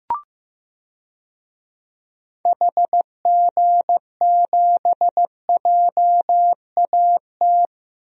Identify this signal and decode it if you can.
Morse code, heard as a keyed tone, about 15 wpm, 700 Hz. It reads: HG7JAT
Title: DXCC-100 Call Signs - 6 Characters - Repeated 2x - 15wpm